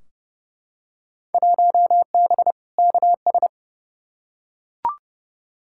Morse code, heard as a keyed tone, about 30 wpm, 700 Hz. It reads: 1B KS
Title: Field Day Exchanges Encore - 30wpm